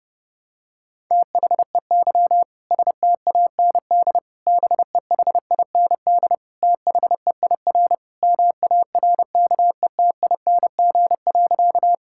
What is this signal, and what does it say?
Morse code, heard as a keyed tone, about 32 wpm, 700 Hz. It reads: THEY STAND BEHIND THEIR MARKETING.
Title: Sentences from Top 1000 Words - Review Part 5 - 30wpm